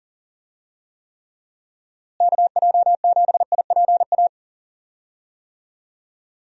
Morse code, heard as a keyed tone, about 40 wpm, 700 Hz. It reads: KJ7IPA
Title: USA Call Signs - 6 Characters - Repeated 2x - 40wpm